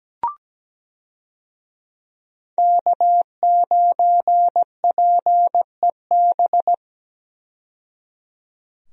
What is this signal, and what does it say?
Morse code, heard as a keyed tone, about 17 wpm, 700 Hz. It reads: K9PEB
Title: USA Call Signs - 5 Characters - 17wpm